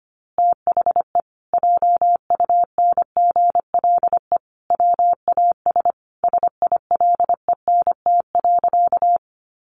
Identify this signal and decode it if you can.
Morse code, heard as a keyed tone, about 25 wpm, 700 Hz. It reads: THE JUNGLE WAS SILENT.
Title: A Sound Of Thunder 25wpm